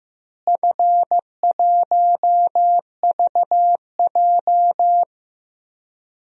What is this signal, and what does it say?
Morse code, heard as a keyed tone, about 15 wpm, 700 Hz. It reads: F1VJ
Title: DXCC-100 Call Signs - 4 Characters - Repeated 3x - 15wpm